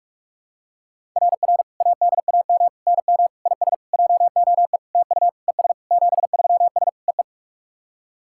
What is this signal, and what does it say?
Morse code, heard as a keyed tone, about 46 wpm, 690 Hz. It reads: RR ADAM NM IS JOE TU ES 73S EE
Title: SOTA QSO Practice 45wpm